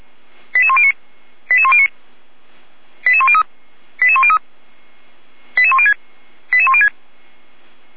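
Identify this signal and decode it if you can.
Five-tone selective call (ZVEI), heard as quick runs of five tones, tones of about 70 ms. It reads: 88109 88109 88103 88103 88106 88106
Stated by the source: ZVEI five-tone call 88192 received by SDR, channel 455.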